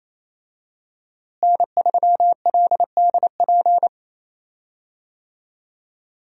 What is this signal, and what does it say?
Morse code, heard as a keyed tone, about 28 wpm, 700 Hz. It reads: N3LDP
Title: USA Call Signs - 5 Characters - Repeated 3x - 28wpm